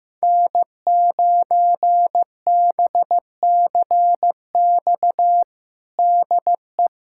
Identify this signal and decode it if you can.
Morse code, heard as a keyed tone, about 16 wpm, 700 Hz. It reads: N9BCX DE
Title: USA Callsign DE Callsign - 5 Characters - 15wpm